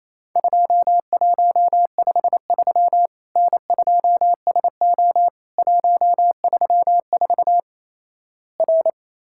Morse code, heard as a keyed tone, about 28 wpm, 700 Hz. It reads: 2153 N2SO 134 R
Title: Worked All Europe DX Contest - QTC Exchange - 28wpm